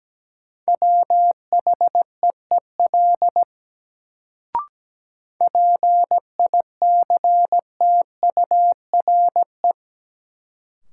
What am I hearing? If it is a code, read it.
Morse code, heard as a keyed tone, about 17 wpm, 700 Hz. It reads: WHEEL PICTURE
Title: Sentences from Top 500 Words - New Word Drill - 17wpm